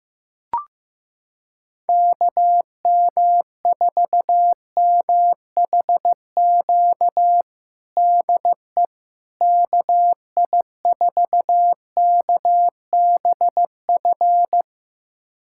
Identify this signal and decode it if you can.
Morse code, heard as a keyed tone, about 15 wpm, 700 Hz. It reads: KM4MHQ DE KI4KBF
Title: USA Callsign DE Callsign - 6 Characters - 15wpm